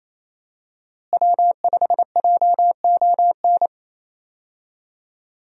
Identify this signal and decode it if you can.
Morse code, heard as a keyed tone, about 27 wpm, 700 Hz. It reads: W5JON
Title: USA Call Signs - 5 Characters - Repeated 2x - 28wpm